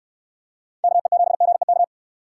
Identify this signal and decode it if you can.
Morse code, heard as a keyed tone, about 67 wpm, 700 Hz. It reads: KE7GEK